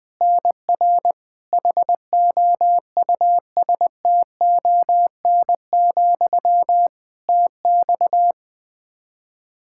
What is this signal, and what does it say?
Morse code, heard as a keyed tone, about 20 wpm, 700 Hz. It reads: NR HOUSTON, TX
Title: K1USN SST QSO Practice 20wpm